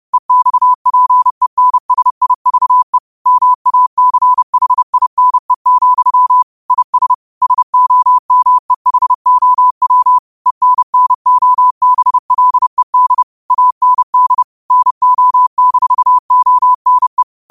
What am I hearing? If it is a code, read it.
Morse code, heard as a keyed tone, about 30 wpm, 1000 Hz. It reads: EXPENSIVE MACHINE, IS SOMEHOW ENNOBLED AND NO-ONE
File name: qotdfiles / 30wpm